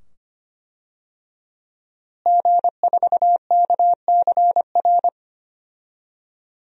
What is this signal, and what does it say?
Morse code, heard as a keyed tone, about 25 wpm, 700 Hz. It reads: G4KCR
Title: DXCC-100 Call Signs - 5 Characters - Repeated 2x - 25wpm